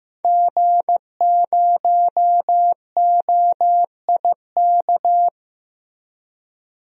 Morse code, heard as a keyed tone, about 15 wpm, 700 Hz. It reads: G0OIK
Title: DXCC-100 Call Signs - 5 Characters - Repeated 2x - 15wpm